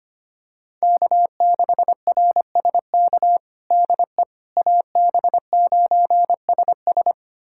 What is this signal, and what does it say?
Morse code, heard as a keyed tone, about 25 wpm, 700 Hz. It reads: K6RSK DE AB9SS
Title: USA Callsign DE Callsign - 5 Characters - 25wpm